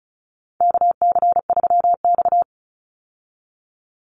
Morse code, heard as a keyed tone, about 34 wpm, 700 Hz. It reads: KC3X
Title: Call Signs Repeated 2x - 35wpm